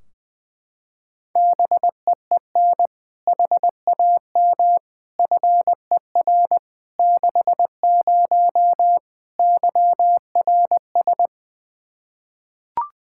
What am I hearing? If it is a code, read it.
Morse code, heard as a keyed tone, about 20 wpm, 700 Hz. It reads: BEEN HAM FER 60 YRS
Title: K1USN SST QSO Practice 20wpm